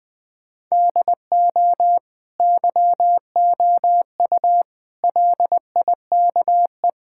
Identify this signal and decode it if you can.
Morse code, heard as a keyed tone, about 19 wpm, 700 Hz. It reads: DO YOU LIKE
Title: Sentences from Top 1000 Words - Review Part 6 - 20wpm